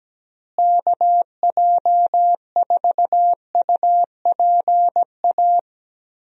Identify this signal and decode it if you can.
Morse code, heard as a keyed tone, about 17 wpm, 700 Hz. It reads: KJ4UPA
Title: USA Call Signs - 6 Characters - Repeated 2x - 17wpm